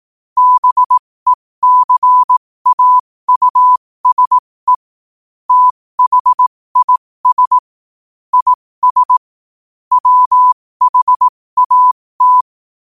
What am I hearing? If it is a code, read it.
Morse code, heard as a keyed tone, about 18 wpm, spaced as for 15 wpm, 1000 Hz. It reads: BECAUSE THIS IS WHAT
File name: qotdfiles / 15wpm